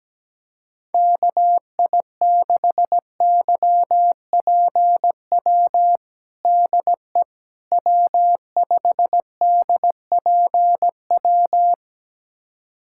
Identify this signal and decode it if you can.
Morse code, heard as a keyed tone, about 17 wpm, 700 Hz. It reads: KI6YPW DE W5DPW
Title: USA Callsign DE Callsign 17wpm